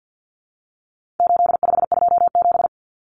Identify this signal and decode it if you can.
Morse code, heard as a keyed tone, about 50 wpm, 700 Hz. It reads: 8527
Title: Common Cut Numbers - 4-Digit 50wpm